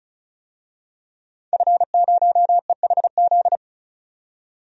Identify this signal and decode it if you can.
Morse code, heard as a keyed tone, about 36 wpm, 700 Hz. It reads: F0EHZ
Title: DXCC-100 Call Signs - 5 Characters - 35wpm